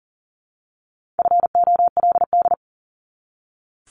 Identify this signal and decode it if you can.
Morse code, heard as a keyed tone, about 40 wpm, 700 Hz. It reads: FOLD
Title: Top 1751-2000 Words 40wpm